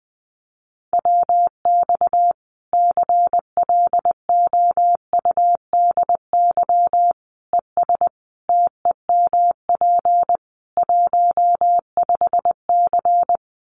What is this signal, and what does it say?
Morse code, heard as a keyed tone, about 19 wpm, 700 Hz. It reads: WX CLOUDY ES TEMP 15C
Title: Common QSO Phrases 20wpm